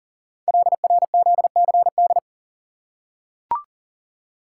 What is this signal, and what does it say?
Morse code, heard as a keyed tone, about 39 wpm, 700 Hz. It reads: LRZCD
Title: Code Groups - 5 Letters - 40wpm